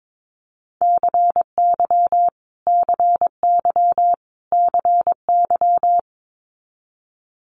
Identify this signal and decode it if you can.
Morse code, heard as a keyed tone, about 22 wpm, 700 Hz. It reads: CY CY CY
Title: Multi-letter Phonemes Repeated 3x 22wpm